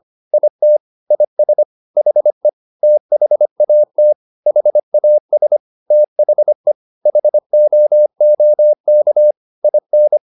Morse code, heard as a keyed tone, about 25 wpm, 600 Hz. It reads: IT IS HE THAT HAS THE HOOK IN
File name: the old man and the sea part 1 25wpm